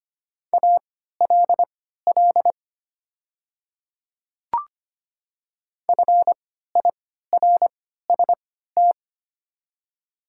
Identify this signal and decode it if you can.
Morse code, heard as a keyed tone, about 25 wpm, spaced as for 14 wpm, 700 Hz. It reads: ALL FIRST
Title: 3x Character Spacing - Top 100 Words - 25wpm